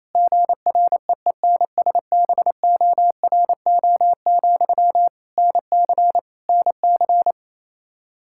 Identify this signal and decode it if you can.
Morse code, heard as a keyed tone, about 28 wpm, 700 Hz. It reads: GREENSBORO, NC NC
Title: Practice QSO - Exchange 1 - Beginning - 28wpm